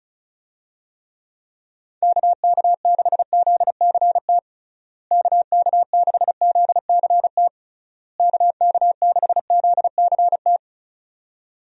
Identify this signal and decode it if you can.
Morse code, heard as a keyed tone, about 36 wpm, 700 Hz. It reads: KK6ZCT KK6ZCT KK6ZCT
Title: USA Call Signs - 6 Characters - Repeated 3x - 35wpm